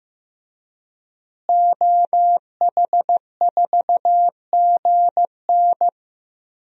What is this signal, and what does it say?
Morse code, heard as a keyed tone, about 16 wpm, 700 Hz. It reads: OH4GN
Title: DXCC-100 Call Signs - 5 Characters - Repeated 3x - 15wpm